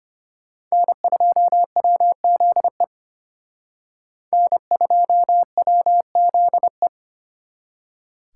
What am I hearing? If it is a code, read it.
Morse code, heard as a keyed tone, about 31 wpm, 700 Hz. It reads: N2WZE N2WZE
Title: Speed Racing - USA 1x3 Call Signs 25wpm